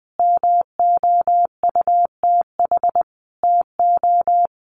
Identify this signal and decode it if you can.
Morse code, heard as a keyed tone, about 20 wpm, 700 Hz. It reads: MOUTH TO
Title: Sentences from Top 1500 Words 20wpm